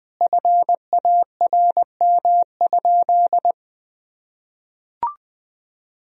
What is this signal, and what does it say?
Morse code, heard as a keyed tone, about 20 wpm, 700 Hz. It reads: FARM?
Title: Sentences from Top 1000 Words - Review Part 3 - 20wpm